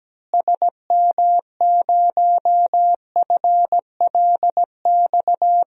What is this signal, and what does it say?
Morse code, heard as a keyed tone, about 17 wpm, 700 Hz. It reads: SM0FLX
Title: DXCC-100 Call Signs - 6 Characters - Repeated 3x - 17wpm